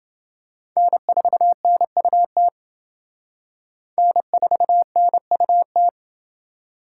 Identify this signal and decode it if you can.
Morse code, heard as a keyed tone, about 29 wpm, 700 Hz. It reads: N4NUT N4NUT